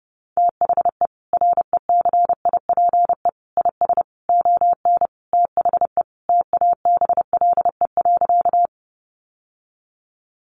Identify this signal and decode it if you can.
Morse code, heard as a keyed tone, about 30 wpm, 700 Hz. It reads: THE RECIPE IS ON THE TABLE.
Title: Sentences from Top 1200 Words 30wpm